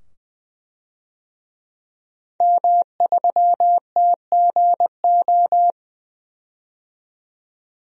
Morse code, heard as a keyed tone, about 20 wpm, 700 Hz. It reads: M3TGO
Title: DXCC-100 Call Signs - 5 Characters - 20wpm